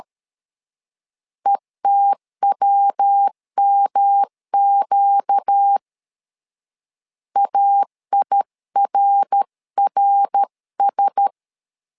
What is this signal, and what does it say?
Morse code, heard as a keyed tone, about 13 wpm, 800 Hz. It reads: ETWMQ AIRRS
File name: LEZIONI / 4 Q1S 12wpm 1802